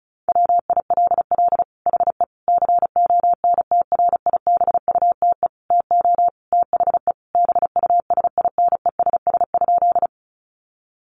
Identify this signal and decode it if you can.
Morse code, heard as a keyed tone, about 34 wpm, 700 Hz. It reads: WILL HE CONTRIBUTE TO THE BUSINESS?
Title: Sentences from Top 1400 Words 35wpm